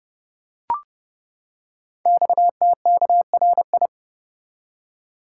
Morse code, heard as a keyed tone, about 29 wpm, 700 Hz. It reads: XTKRI
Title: Code Groups - 5 Letters - 30wpm